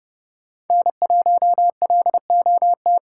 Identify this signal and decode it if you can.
Morse code, heard as a keyed tone, about 29 wpm, 700 Hz. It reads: N1LOT